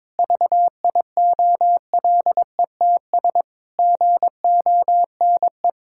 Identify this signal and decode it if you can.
Morse code, heard as a keyed tone, about 22 wpm, 700 Hz. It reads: VIOLETS GONE
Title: Pangram Challenge v2 - 22wpm